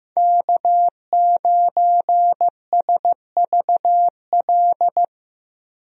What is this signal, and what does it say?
Morse code, heard as a keyed tone, about 15 wpm, 700 Hz. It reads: K9SVL
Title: USA Call Signs - 5 Characters - Repeated 3x - 15wpm